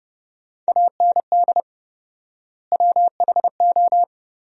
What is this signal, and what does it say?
Morse code, heard as a keyed tone, about 30 wpm, 700 Hz. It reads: AND WHO
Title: 4x Word-Spacing - Sentences from Top 100 Words - 30wpm